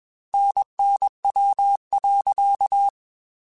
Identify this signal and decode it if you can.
Morse code, heard as a keyed tone, about 21 wpm, 800 Hz. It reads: NNW.
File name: Weather Sets / Weather Sets-mp3s / Weather 5 22 @20